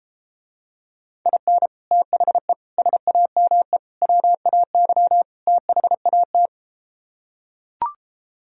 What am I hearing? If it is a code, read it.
Morse code, heard as a keyed tone, about 33 wpm, 700 Hz. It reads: IN THE SAME WAY THAT